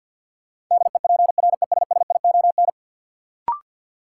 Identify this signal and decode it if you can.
Morse code, heard as a keyed tone, about 53 wpm, 700 Hz. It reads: DEPRESSION
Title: Sentences from Top 1300 Words - New Word Drill Alt - 50wpm